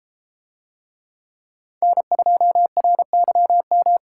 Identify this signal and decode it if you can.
Morse code, heard as a keyed tone, about 32 wpm, 700 Hz. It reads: N2RYM